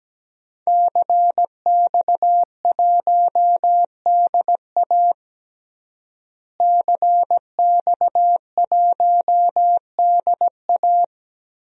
Morse code, heard as a keyed tone, about 17 wpm, 700 Hz. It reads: CX1DA CX1DA
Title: DXCC-100 Call Signs - 5 Characters - Repeated 3x - 17wpm